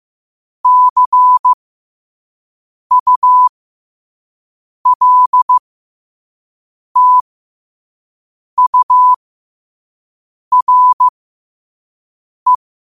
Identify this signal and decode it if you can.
Morse code, heard as a keyed tone, about 15 wpm, spaced as for 5 wpm, 1000 Hz. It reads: CULTURE
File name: qotdfiles / 5wpm